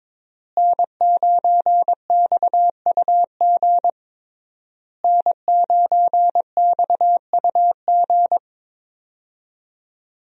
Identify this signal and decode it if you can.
Morse code, heard as a keyed tone, about 22 wpm, 700 Hz. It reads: N9XUG N9XUG
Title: USA Call Signs - 5 Characters - Repeated 3x - 22wpm